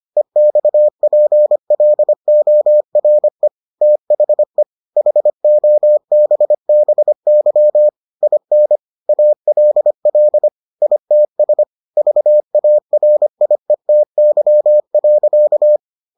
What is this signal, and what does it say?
Morse code, heard as a keyed tone, about 25 wpm, 600 Hz. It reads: EXPLORE THE HOBBY IN ALL ITS VARIETY.